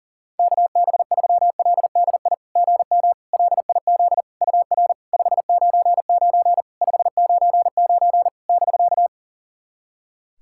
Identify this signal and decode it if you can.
Morse code, heard as a keyed tone, about 44 wpm, 700 Hz. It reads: KB3LDI GM LIZ UR 599 599 <BK>
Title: SOTA QSO Practice 40wpm